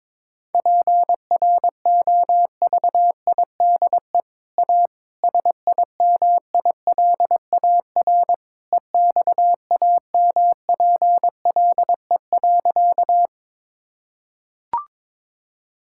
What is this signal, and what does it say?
Morse code, heard as a keyed tone, about 21 wpm, 700 Hz. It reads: PROVIDE A SIMILAR EXAMPLE.
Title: Sentences from Top 1000 Words - Review Part 5 - 22wpm